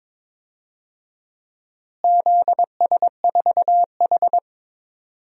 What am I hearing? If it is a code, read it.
Morse code, heard as a keyed tone, about 22 wpm, 700 Hz. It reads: ZS4H
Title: DXCC-100 Call Signs - 4 Characters - Repeated 3x - 22wpm